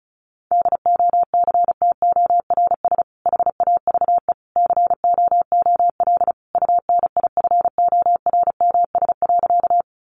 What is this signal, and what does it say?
Morse code, heard as a keyed tone, about 34 wpm, 700 Hz. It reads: DOCTORS HAVE COOL UNIFORMS.
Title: Easy Phrases 35wpm